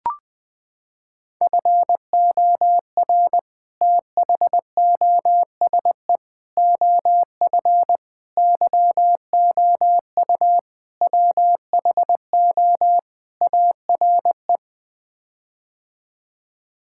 Morse code, heard as a keyed tone, about 20 wpm, 700 Hz. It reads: FOR THOSE OF YOU WHO ARE
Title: Sets of 6 Words - 20wpm